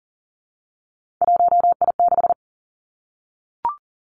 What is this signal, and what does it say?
Morse code, heard as a keyed tone, about 38 wpm, 700 Hz. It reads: 1I6
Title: Three Challenging Letter-Number Combinations 40wpm